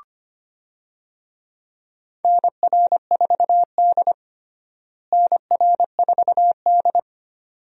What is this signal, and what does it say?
Morse code, heard as a keyed tone, about 25 wpm, 700 Hz. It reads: NR4D NR4D
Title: USA Call Signs - 4 Characters - Repeated 2x - 25wpm